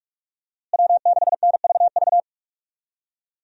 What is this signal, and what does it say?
Morse code, heard as a keyed tone, about 46 wpm, 700 Hz. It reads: W6NVV
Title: USA Call Signs - 5 Characters - Repeated 3x - 45wpm